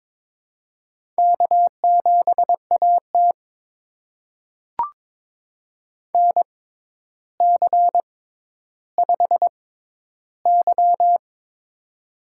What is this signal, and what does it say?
Morse code, heard as a keyed tone, about 22 wpm, spaced as for 11 wpm, 700 Hz. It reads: K7AT NC5Y
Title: USA Call Signs - 4 Characters - 6x Character Spacing - 22wpm